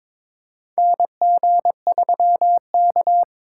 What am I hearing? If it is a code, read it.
Morse code, heard as a keyed tone, about 22 wpm, 700 Hz. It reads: NG3K
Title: USA Call Signs - 4x Character Spacing - 22wpm